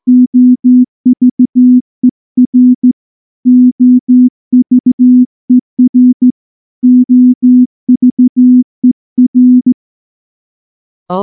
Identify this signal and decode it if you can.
Morse code, heard as a keyed tone, about 14 wpm, 250 Hz. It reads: OVER OVER OVER
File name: morse-training-15WPM